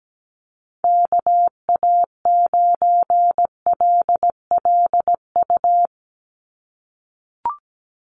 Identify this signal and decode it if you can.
Morse code, heard as a keyed tone, about 17 wpm, 700 Hz. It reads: KA9LLU
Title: USA Call Signs 17wpm